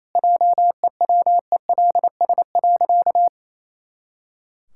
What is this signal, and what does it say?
Morse code, heard as a keyed tone, about 28 wpm, 700 Hz. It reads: JEWELS.
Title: Pangram Challenge v2 - 28wpm